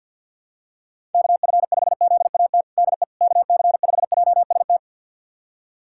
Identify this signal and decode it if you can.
Morse code, heard as a keyed tone, about 51 wpm, 700 Hz. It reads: KF5ZAT DE KC5JST
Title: USA Callsign DE Callsign - 6 Characters - 50wpm